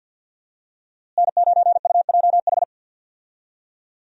Morse code, heard as a keyed tone, about 47 wpm, 700 Hz. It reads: N9UJH
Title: USA Call Signs - 5 Characters - Repeated 3x - 50wpm